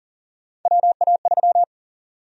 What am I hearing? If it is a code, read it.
Morse code, heard as a keyed tone, about 40 wpm, 700 Hz. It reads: WA3
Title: USA Call Sign Prefixes - Repeated 2x - 40wpm